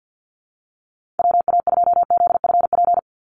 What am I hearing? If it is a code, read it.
Morse code, heard as a keyed tone, about 50 wpm, 700 Hz. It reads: WA2ZFP
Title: USA Call Signs 50wpm